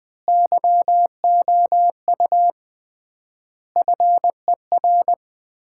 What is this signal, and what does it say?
Morse code, heard as a keyed tone, about 20 wpm, 700 Hz. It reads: YOU FER
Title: Common QSO Phrases - 3x Word-Spacing - 20wpm